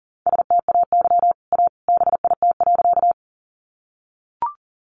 Morse code, heard as a keyed tone, about 40 wpm, 700 Hz. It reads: STAY A BIT.
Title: Sentences from Top 900 Words 40wpm